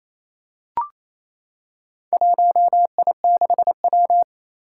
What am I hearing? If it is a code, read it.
Morse code, heard as a keyed tone, about 28 wpm, 700 Hz. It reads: 1I6W
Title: Four Challenging Letter-Number Combinations 28wpm